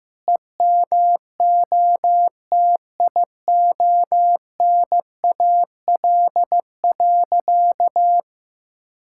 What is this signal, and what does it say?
Morse code, heard as a keyed tone, about 15 wpm, 700 Hz. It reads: EMOTIONAL.
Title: Sentences from Top 1000 Words - Review Part 5 - 15wpm